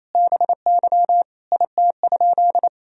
Morse code, heard as a keyed tone, about 28 wpm, 700 Hz. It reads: BY IT?
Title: Sentences from Top 100 Words and Abbreviations 28wpm